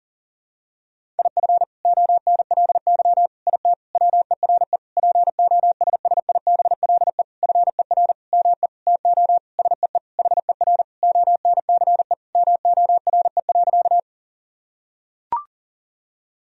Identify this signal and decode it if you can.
Morse code, heard as a keyed tone, about 40 wpm, 700 Hz. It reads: IF ONLY IT WERE POSSIBLE FER ME TO SEE HER ONCE MORE.
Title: Sentences from Top 500 Words and Abbreviations 40wpm